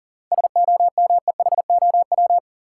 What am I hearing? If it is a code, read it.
Morse code, heard as a keyed tone, about 40 wpm, 700 Hz. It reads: SOMEHOW
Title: Seven Letter Words - 40wpm